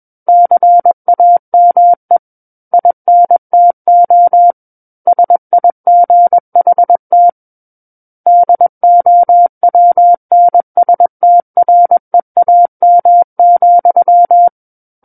Morse code, heard as a keyed tone, about 21 wpm, 700 Hz. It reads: CAME INTO SIGHT DOWNSTREAM,
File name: Book One / 12-Chapter 12 - 21 WPM